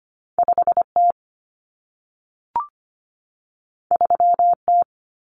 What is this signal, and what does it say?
Morse code, heard as a keyed tone, about 25 wpm, 700 Hz. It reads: 5T 3T
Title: Common Cut Numbers - 2-Digit 25wpm